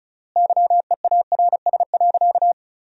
Morse code, heard as a keyed tone, about 35 wpm, 700 Hz. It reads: YEARS.
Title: 3x Word-Spacing - Sentences from Top 200 Words - 35wpm